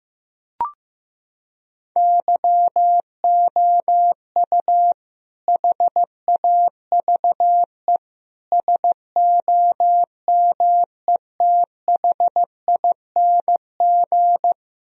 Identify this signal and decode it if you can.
Morse code, heard as a keyed tone, about 15 wpm, 700 Hz. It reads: YOU HAVE SOMETHING
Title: Sentences from Top 1000 Words - Review Part 4 - 15wpm